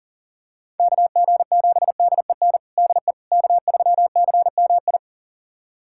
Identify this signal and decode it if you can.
Morse code, heard as a keyed tone, about 40 wpm, 700 Hz. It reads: KG7DEN DE K3CMI
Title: USA Callsign DE Callsign 40wpm